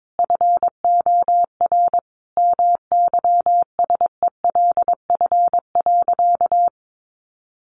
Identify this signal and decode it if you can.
Morse code, heard as a keyed tone, about 22 wpm, 700 Hz. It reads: FOR MYSELF.
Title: Sentences from Top 1300 Words 22wpm